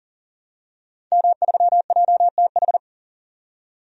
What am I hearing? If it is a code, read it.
Morse code, heard as a keyed tone, about 40 wpm, 700 Hz. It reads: M3JTH
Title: DXCC-100 Call Signs - 5 Characters - 40wpm